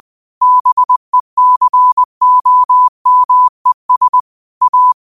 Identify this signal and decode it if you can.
Morse code, heard as a keyed tone, about 20 wpm, 1000 Hz. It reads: BECOMES A
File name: qotdfiles / 20wpm